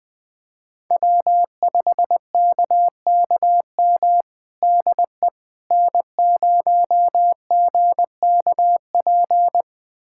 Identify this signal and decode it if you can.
Morse code, heard as a keyed tone, about 20 wpm, 700 Hz. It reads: W5KKM DE N0GKP
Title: USA Callsign DE Callsign - 5 Characters - 20wpm